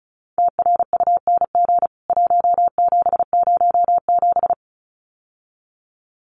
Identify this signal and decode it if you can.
Morse code, heard as a keyed tone, about 34 wpm, 700 Hz. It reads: TRUNG 1707
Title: CWT Exchange 35wpm